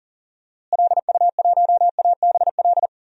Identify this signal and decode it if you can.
Morse code, heard as a keyed tone, about 40 wpm, 700 Hz. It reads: LU1ABL
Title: DXCC-100 Call Signs - 6 Characters - Repeated 3x - 40wpm